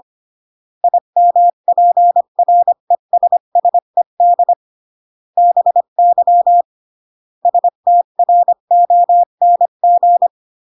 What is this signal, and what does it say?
Morse code, heard as a keyed tone, about 25 wpm, 700 Hz. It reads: IMPRESSED BY STRONG
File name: CW / jodorowsky / 82 commandments 25-20